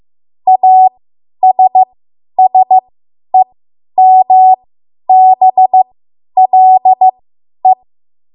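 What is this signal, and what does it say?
Morse code, heard as a keyed tone, about 15 wpm, spaced as for 10 wpm, 750 Hz. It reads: ASSEMBLE
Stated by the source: Morse code practice run at 10 WPM from a code practice archive.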